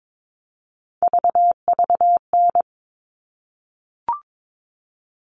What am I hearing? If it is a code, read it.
Morse code, heard as a keyed tone, about 22 wpm, 700 Hz. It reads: VVN
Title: All Cut Numbers - 3-Digit 22wpm